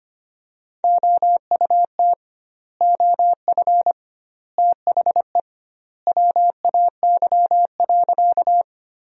Morse code, heard as a keyed tone, about 25 wpm, 700 Hz. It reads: OUT OF THE WAY.
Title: 2x Word-Spacing - Sentences from Top 100 Words - 25wpm